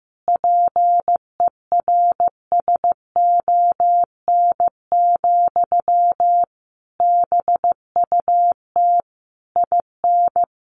Morse code, heard as a keyed tone, about 15 wpm, 700 Hz. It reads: PERSON, BUT IN
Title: Sentences from Top 1400 Words 15wpm